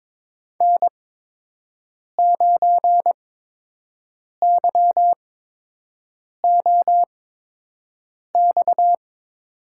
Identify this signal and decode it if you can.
Morse code, heard as a keyed tone, about 22 wpm, spaced as for 6 wpm, 700 Hz. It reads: N9YOX
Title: USA Call Signs - 5 Characters - 8x Character Spacing - 22wpm